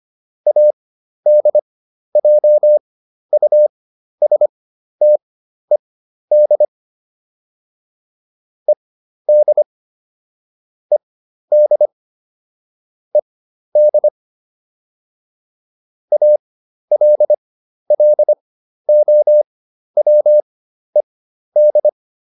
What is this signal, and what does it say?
Morse code, heard as a keyed tone, about 25 wpm, spaced as for 12 wpm, 600 Hz. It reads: ADJUSTED ED ED ED ALLOWED